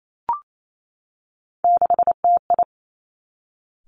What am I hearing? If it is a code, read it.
Morse code, heard as a keyed tone, about 28 wpm, 700 Hz. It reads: 6TI
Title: Three Challenging Letter-Number Combinations 28wpm